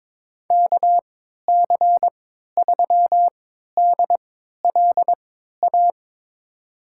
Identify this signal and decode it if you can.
Morse code, heard as a keyed tone, about 22 wpm, spaced as for 13 wpm, 700 Hz. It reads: KC3DLA
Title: USA Call Signs - 6 Characters - 3x Character Spacing - 22wpm